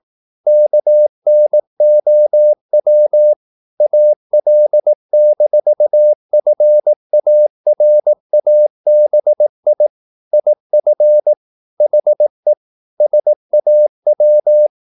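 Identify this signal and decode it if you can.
Morse code, heard as a keyed tone, about 18 wpm, 600 Hz. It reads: KNOW AL-FARABI IF HE SAW
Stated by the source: Morse practice at 18 wpm, English text, from CW med Gnister.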